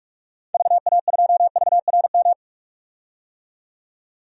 Morse code, heard as a keyed tone, about 45 wpm, 700 Hz. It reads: VA2VRM
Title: DXCC-100 Call Signs - 6 Characters - 45wpm